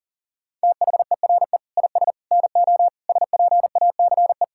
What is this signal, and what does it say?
Morse code, heard as a keyed tone, about 40 wpm, 700 Hz. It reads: THERE IS NO SPACE
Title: Sentences from Top 1000 Words - Review Part 5 - 40wpm